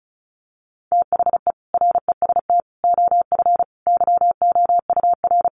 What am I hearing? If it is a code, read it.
Morse code, heard as a keyed tone, about 34 wpm, 700 Hz. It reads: THE REST OF YOUR
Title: Sets of 5 Words 35wpm